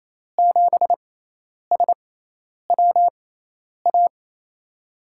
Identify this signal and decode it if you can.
Morse code, heard as a keyed tone, about 27 wpm, spaced as for 10 wpm, 700 Hz. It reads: 7SWA
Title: USA Call Signs - 6x Character Spacing - 28wpm